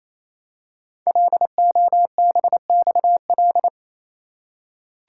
Morse code, heard as a keyed tone, about 27 wpm, 700 Hz. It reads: LOBXL
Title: Code Groups - 5 Letters - 28wpm